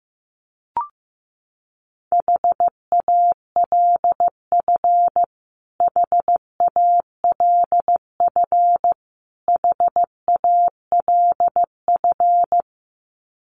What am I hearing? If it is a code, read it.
Morse code, heard as a keyed tone, about 15 wpm, 700 Hz. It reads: HALF HALF HALF
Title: Top 500 Words Repeated 15wpm